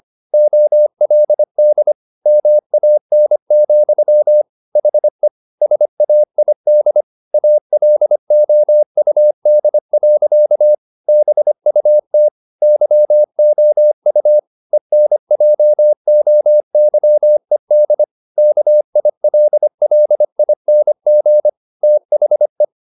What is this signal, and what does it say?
Morse code, heard as a keyed tone, about 25 wpm, 600 Hz. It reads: OLD MAN, HE SAID ALOUD. BUT YOU ENJOYED KILLING THE